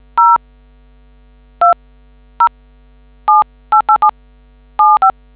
DTMF, heard as telephone touch tones, tones of about 110 ms. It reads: 0 20788775